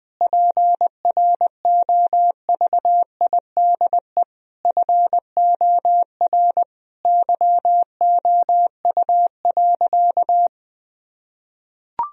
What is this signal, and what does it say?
Morse code, heard as a keyed tone, about 20 wpm, 700 Hz. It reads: PROVIDE FOR YOU.
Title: Sentences from Top 1000 Words - Review Part 3 - 20wpm